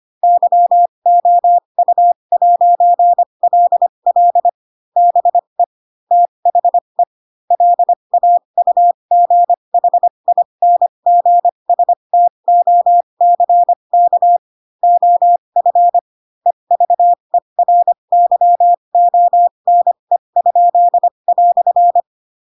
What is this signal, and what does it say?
Morse code, heard as a keyed tone, about 25 wpm, spaced as for 22 wpm, 700 Hz. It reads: YOU'LL BE THE LAUGHINGSTOCK OF EVERYONE?"
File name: Pinocchio / Pinocchio ch4 18 25 ik1hge